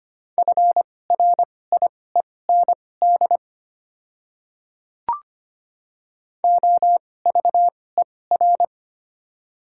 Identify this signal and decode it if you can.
Morse code, heard as a keyed tone, about 25 wpm, spaced as for 18 wpm, 700 Hz. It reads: FRIEND OVER
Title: 2x Character Spacing - Top 300 Words - 25wpm